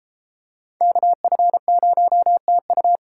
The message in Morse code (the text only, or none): KF0TU